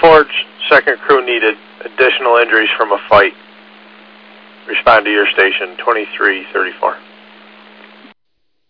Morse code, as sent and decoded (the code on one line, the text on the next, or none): none